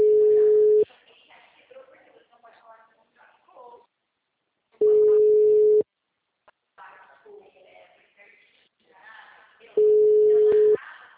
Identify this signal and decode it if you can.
Ringback tone (Europe, 425 Hz), with speech.